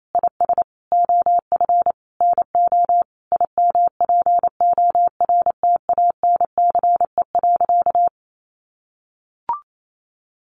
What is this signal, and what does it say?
Morse code, heard as a keyed tone, about 28 wpm, 700 Hz. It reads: IS OF NO IMPORTANCE.
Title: Sentences from Top 1200 Words 28wpm